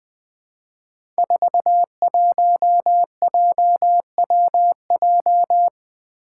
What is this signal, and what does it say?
Morse code, heard as a keyed tone, about 20 wpm, 700 Hz. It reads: 41JWJ
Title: Five Challenging Letter-Number Combinations 20wpm